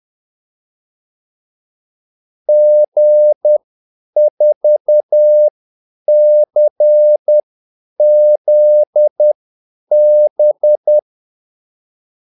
Morse code, heard as a keyed tone, about 10 wpm, spaced as for 8 wpm, 600 Hz. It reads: G4CZB
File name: Callsigns-10-8wpm